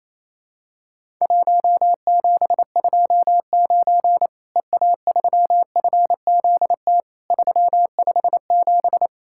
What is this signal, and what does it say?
Morse code, heard as a keyed tone, about 28 wpm, 700 Hz. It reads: 1729 EA3FZT 357
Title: Worked All Europe DX Contest - QTC Exchange - 28wpm